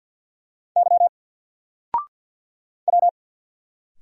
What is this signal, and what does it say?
Morse code, heard as a keyed tone, about 50 wpm, 700 Hz. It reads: Y W
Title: Single Letter-Number Alt 50wpm